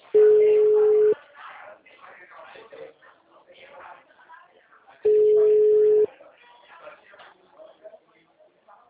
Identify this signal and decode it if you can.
Ringback tone (Europe, 425 Hz), with speech.